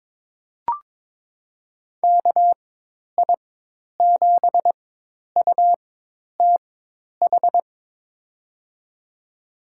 Morse code, heard as a keyed tone, about 21 wpm, spaced as for 10 wpm, 700 Hz. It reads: KI7UTH
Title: USA Call Signs - 6 Characters - 4x Character Spacing - 22wpm